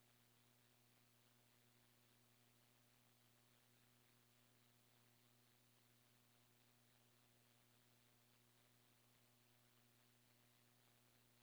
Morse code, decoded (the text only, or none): none